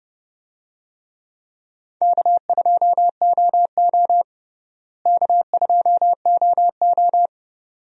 Morse code, heard as a keyed tone, about 30 wpm, 700 Hz. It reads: K2OO K2OO
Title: USA Call Signs - 4 Characters - Repeated 2x - 30wpm